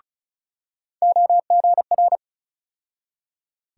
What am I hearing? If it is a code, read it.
Morse code, heard as a keyed tone, about 36 wpm, 700 Hz. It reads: OGR
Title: USA Call Sign Postfixes - 3 Letters 35wpm